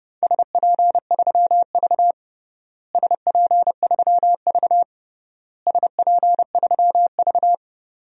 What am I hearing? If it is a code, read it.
Morse code, heard as a keyed tone, about 30 wpm, 700 Hz. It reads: SP3V SP3V SP3V
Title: DXCC-100 Call Signs - 4 Characters - Repeated 3x - 30wpm